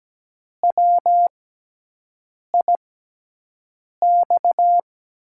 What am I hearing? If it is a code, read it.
Morse code, heard as a keyed tone, about 17 wpm, spaced as for 6 wpm, 700 Hz. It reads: WIX
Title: USA Call Signs - 6x Character Spacing - 17wpm